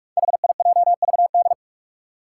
Morse code, heard as a keyed tone, about 44 wpm, 700 Hz. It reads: HIJVD
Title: Five Challenging Letter-Number Combinations 45wpm